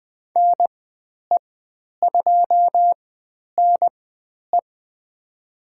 Morse code, heard as a keyed tone, about 20 wpm, spaced as for 10 wpm, 700 Hz. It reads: NE2NE
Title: 102 Lesson 17g - Letter u - Call Signs 20wpm